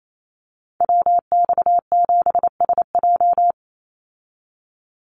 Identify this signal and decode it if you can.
Morse code, heard as a keyed tone, about 28 wpm, 700 Hz. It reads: WX7SJ
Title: Call Signs Repeated 2x - 28wpm